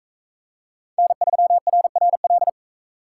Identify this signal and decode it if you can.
Morse code, heard as a keyed tone, about 44 wpm, 700 Hz. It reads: N3RRL